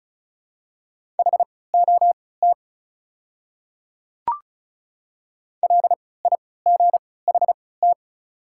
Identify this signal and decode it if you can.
Morse code, heard as a keyed tone, about 36 wpm, spaced as for 20 wpm, 700 Hz. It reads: HOT LIGHT
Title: 3x Character Spacing - Top 200 Words - 35wpm